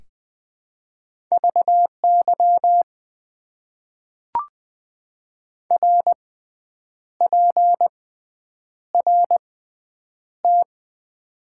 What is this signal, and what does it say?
Morse code, heard as a keyed tone, about 20 wpm, spaced as for 8 wpm, 700 Hz. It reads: VY RPRT
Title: Top 90 QSO Elements - 6x Character Spacing - 20wpm